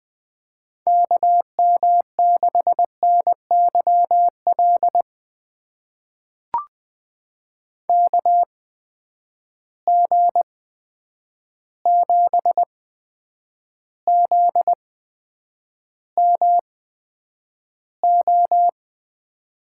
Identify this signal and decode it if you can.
Morse code, heard as a keyed tone, about 21 wpm, spaced as for 9 wpm, 700 Hz. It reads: KM6NYL KG7ZMO
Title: USA Call Signs - 6 Characters - 8x Character Spacing - 20wpm